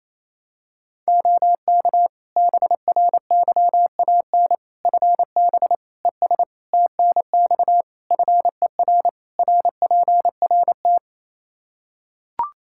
Morse code, heard as a keyed tone, about 28 wpm, 700 Hz. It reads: OK BRYAN FB ES TNX FER RPRT
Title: Practice QSO - Exchange 2 - Info - 28wpm